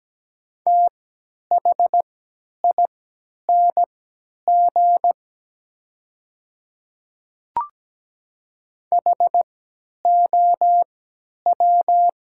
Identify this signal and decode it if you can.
Morse code, heard as a keyed tone, about 17 wpm, spaced as for 10 wpm, 700 Hz. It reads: THING HOW
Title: 3x Character Spacing - Top 200 Words - 17wpm